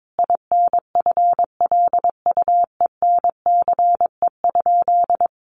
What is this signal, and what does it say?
Morse code, heard as a keyed tone, about 22 wpm, 700 Hz. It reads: INFLUENCE?
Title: Sentences from Top 1100 Words 22wpm